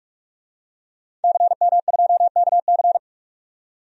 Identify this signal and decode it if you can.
Morse code, heard as a keyed tone, about 48 wpm, 700 Hz. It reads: CM2KC
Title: DXCC-100 Call Signs - 5 Characters - 45wpm